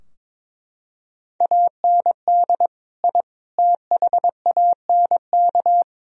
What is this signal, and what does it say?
Morse code, heard as a keyed tone, about 22 wpm, 700 Hz. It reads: AND I THANK
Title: Sentences from Top 1000 Words - Review Part 3 - 22wpm